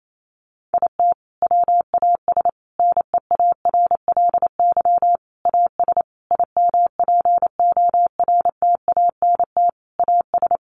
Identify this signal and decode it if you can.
Morse code, heard as a keyed tone, about 28 wpm, 700 Hz. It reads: IT WAS NEARLY AS IMPORTANT AS
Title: Sentences from Top 1400 Words 28wpm